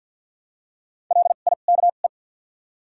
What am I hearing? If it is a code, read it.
Morse code, heard as a keyed tone, about 53 wpm, 700 Hz. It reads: L I K E
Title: 2x Character Spacing - Top 300 Words - 50wpm